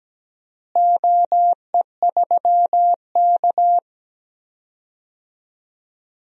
Morse code, heard as a keyed tone, about 17 wpm, 700 Hz. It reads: OE3K
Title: DXCC-100 Call Signs - 4 Characters - Repeated 3x - 17wpm